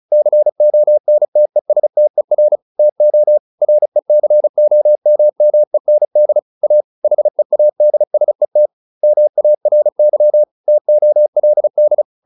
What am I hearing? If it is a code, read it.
Morse code, heard as a keyed tone, about 35 wpm, 600 Hz. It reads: CONTESTER TO RECOMMEND A HEADSET MARY TOLD
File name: SS303 35